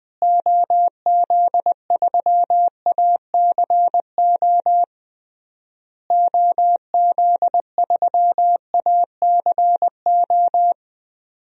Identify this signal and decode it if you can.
Morse code, heard as a keyed tone, about 20 wpm, 700 Hz. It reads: OZ3ACO OZ3ACO
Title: DXCC-100 Call Signs - 6 Characters - Repeated 3x - 20wpm